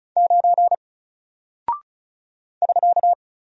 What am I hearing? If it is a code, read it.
Morse code, heard as a keyed tone, about 35 wpm, 700 Hz. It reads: 9 <SK>
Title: Single Character (Letter:Number:Punctuation:Prosign) 35wpm